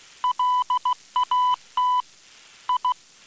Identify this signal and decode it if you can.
Morse code, heard as a keyed tone, about 16 wpm, 1000 Hz. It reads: LAT I